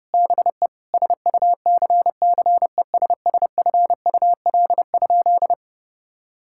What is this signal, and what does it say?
Morse code, heard as a keyed tone, about 30 wpm, 700 Hz. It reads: BE SUCCESSFUL?
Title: Sentences from Top 1000 Words - Review Part 5 - 30wpm